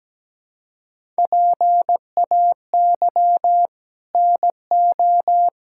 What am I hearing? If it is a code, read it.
Morse code, heard as a keyed tone, about 17 wpm, 700 Hz. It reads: PAY NO
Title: Sentences from Top 1000 Words - Review Part 6 - 17wpm